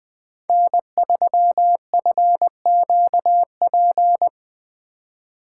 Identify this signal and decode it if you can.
Morse code, heard as a keyed tone, about 20 wpm, 700 Hz. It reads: N3FQP
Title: USA Call Signs - 5 Characters - 2x Character Spacing - 20wpm